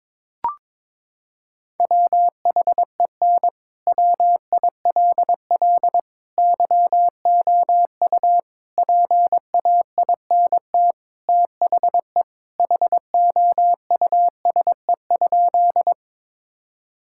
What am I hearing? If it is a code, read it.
Morse code, heard as a keyed tone, about 22 wpm, 700 Hz. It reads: WHEN WILL YOU PAINT THE HOUSE?
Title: Sentences from Top 1000 Words - Review Part 6 - 22wpm